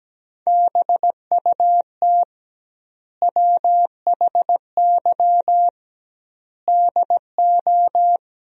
Morse code, heard as a keyed tone, about 17 wpm, 700 Hz. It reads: BUT WHY DO
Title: 2x Word-Spacing - Sentences from Top 200 Words - 17wpm